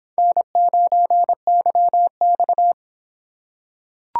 Morse code, heard as a keyed tone, about 25 wpm, 700 Hz. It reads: N9YX